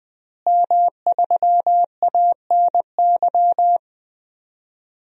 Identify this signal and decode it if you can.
Morse code, heard as a keyed tone, about 21 wpm, 700 Hz. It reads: M3ANY
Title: DXCC-100 Call Signs - 5 Characters - Repeated 3x - 20wpm